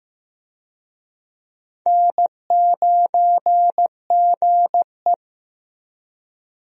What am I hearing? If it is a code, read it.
Morse code, heard as a keyed tone, about 15 wpm, 700 Hz. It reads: N9GE
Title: USA Call Signs - 4 Characters - 15wpm